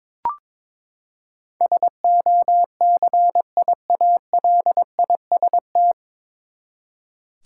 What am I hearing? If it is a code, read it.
Morse code, heard as a keyed tone, about 22 wpm, 700 Hz. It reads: SOCIALIST
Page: Morse Code Ninja practice 22wpm